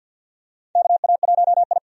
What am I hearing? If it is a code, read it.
Morse code, heard as a keyed tone, about 52 wpm, 700 Hz. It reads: KA1I
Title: USA Call Signs - 4 Characters - 50wpm